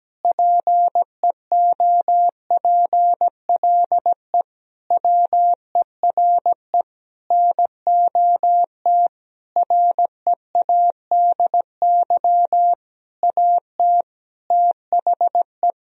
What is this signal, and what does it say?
Morse code, heard as a keyed tone, about 17 wpm, 700 Hz. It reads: PEOPLE WERE NOT READY AT THE
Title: Sentences from Top 1000 Words - Review Part 2 - 17wpm